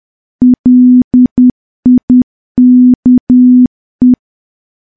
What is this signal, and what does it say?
Morse code, heard as a keyed tone, about 10 wpm, 250 Hz. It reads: LIKE
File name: morse / morse-training-10WPM